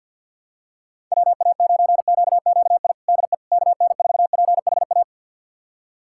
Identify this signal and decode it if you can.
Morse code, heard as a keyed tone, about 51 wpm, 700 Hz. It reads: WA9QQI DE KN4PHA
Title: USA Callsign DE Callsign 50wpm